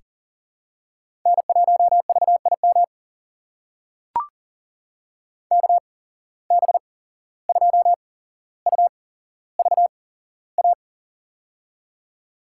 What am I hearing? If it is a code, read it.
Morse code, heard as a keyed tone, about 40 wpm, spaced as for 16 wpm, 700 Hz. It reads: N1VIM KB2UVA
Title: USA Call Signs - 8x Character Spacing - 40wpm